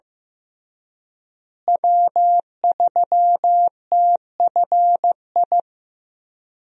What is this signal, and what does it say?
Morse code, heard as a keyed tone, about 14 wpm, 700 Hz. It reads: W3TFI
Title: USA Call Signs - 5 Characters - Repeated 3x - 15wpm